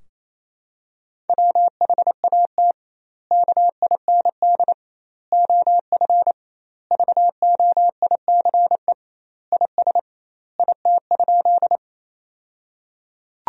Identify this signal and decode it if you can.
Morse code, heard as a keyed tone, about 28 wpm, 700 Hz. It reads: WHAT KIND OF VOICE IS IT?
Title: 2x Word-Spacing - Sentences from Top 500 Words - 28wpm